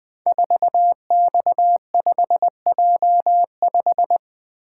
Morse code, heard as a keyed tone, about 20 wpm, 700 Hz. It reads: 4X5J5
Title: Code Groups - 5 Letter:Numbers - 20wpm